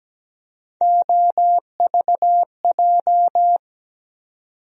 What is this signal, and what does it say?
Morse code, heard as a keyed tone, about 17 wpm, 700 Hz. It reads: OVJ
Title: USA Call Sign Postfixes - 3 Letters 17wpm